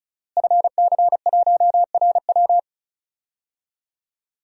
Morse code, heard as a keyed tone, about 36 wpm, 700 Hz. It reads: FC1RW
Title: DXCC-100 Call Signs - 5 Characters - Repeated 3x - 35wpm